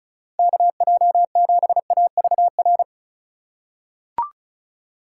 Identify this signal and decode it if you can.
Morse code, heard as a keyed tone, about 36 wpm, 700 Hz. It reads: KJ7AVR
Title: USA Call Signs - 2x Character Spacing - 35wpm